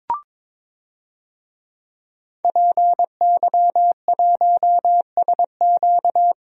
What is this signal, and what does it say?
Morse code, heard as a keyed tone, about 22 wpm, 700 Hz. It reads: PY1SQ
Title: DXCC-100 Call Signs - 5 Characters - 22wpm